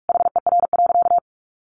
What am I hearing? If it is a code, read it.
Morse code, heard as a keyed tone, about 44 wpm, 700 Hz. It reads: HER.
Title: Sentences from Top 900 Words 45wpm